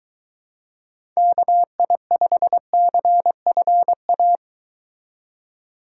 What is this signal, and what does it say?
Morse code, heard as a keyed tone, about 22 wpm, 700 Hz. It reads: KI5CFA